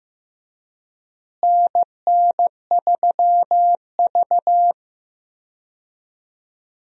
Morse code, heard as a keyed tone, about 15 wpm, 700 Hz. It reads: NN3V
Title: USA Call Signs - 4 Characters - Repeated 2x - 15wpm